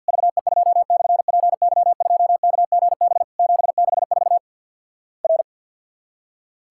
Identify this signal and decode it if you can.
Morse code, heard as a keyed tone, about 50 wpm, 700 Hz. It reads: VE2/PY2KGB 764 R
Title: Worked All Europe DX Contest - QTC Exchange with Repeats - 50wpm